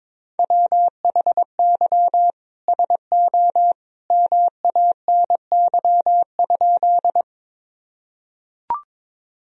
Morse code, heard as a keyed tone, about 21 wpm, 700 Hz. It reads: WHY SO MANY?
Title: Sentences from Top 1000 Words - Review Part 1 - 22wpm